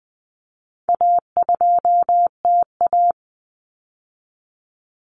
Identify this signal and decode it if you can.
Morse code, heard as a keyed tone, about 20 wpm, 700 Hz. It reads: A2TA
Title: Common Cut Numbers - 4-Digit 20wpm